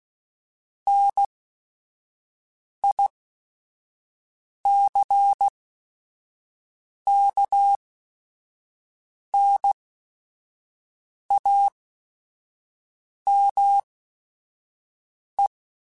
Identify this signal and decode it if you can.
Morse code, heard as a keyed tone, about 16 wpm, spaced as for 5 wpm, 800 Hz. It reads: NICKNAME
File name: Technical Subjects / Technical Subjects-mp3s / OR Gate @5